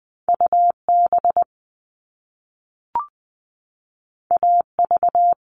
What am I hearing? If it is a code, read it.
Morse code, heard as a keyed tone, about 20 wpm, 700 Hz. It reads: UB AV
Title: All Cut Numbers - 2-Digit 20wpm